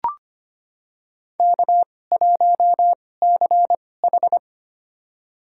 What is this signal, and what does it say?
Morse code, heard as a keyed tone, about 25 wpm, spaced as for 18 wpm, 700 Hz. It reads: K1CH
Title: USA Call Signs - 2x Character Spacing - 25wpm